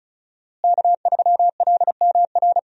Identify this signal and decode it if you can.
Morse code, heard as a keyed tone, about 36 wpm, 700 Hz. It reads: K3LMR
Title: USA Call Signs - 5 Characters - Repeated 2x - 35wpm